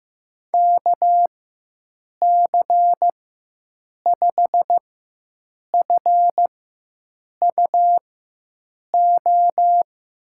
Morse code, heard as a keyed tone, about 15 wpm, spaced as for 7 wpm, 700 Hz. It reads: KC5FUO
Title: USA Call Signs - 4x Character Spacing - 15wpm